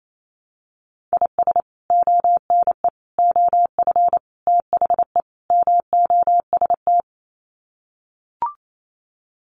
Morse code, heard as a keyed tone, about 28 wpm, 700 Hz. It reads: IS ONE OF THE MOST
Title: Sets of 5 Words 28wpm